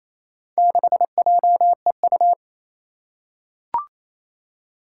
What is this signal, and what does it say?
Morse code, heard as a keyed tone, about 27 wpm, 700 Hz. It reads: 6JEU
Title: Four Challenging Letter-Number Combinations 28wpm